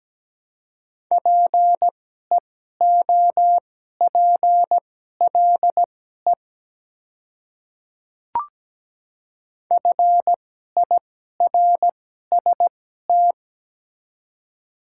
Morse code, heard as a keyed tone, about 17 wpm, spaced as for 12 wpm, 690 Hz. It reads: PEOPLE FIRST
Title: 2x Character Spacing - Top 100 Words - 17wpm